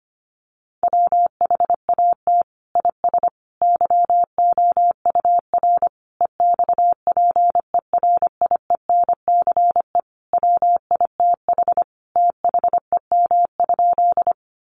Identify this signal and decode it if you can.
Morse code, heard as a keyed tone, about 25 wpm, 700 Hz. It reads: WHAT IS YOUR EXPERIENCE WITH THEM?
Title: Sentences from Top 600 Words 25wpm